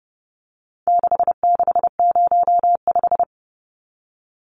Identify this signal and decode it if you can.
Morse code, heard as a keyed tone, about 30 wpm, 700 Hz. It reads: 6605
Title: Numbers - 4-Digit 30wpm